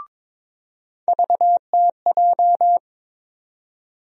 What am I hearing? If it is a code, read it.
Morse code, heard as a keyed tone, about 22 wpm, 700 Hz. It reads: VTJ
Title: USA Call Sign Postfixes - 3 Letters 22wpm